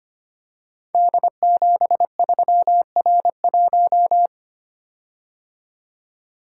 Morse code, heard as a keyed tone, about 25 wpm, 700 Hz. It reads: D73R1
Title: Code Groups - 5 Letter:Numbers - 25wpm